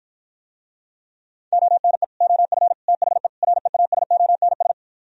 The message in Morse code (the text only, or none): ONE OF THE REASONS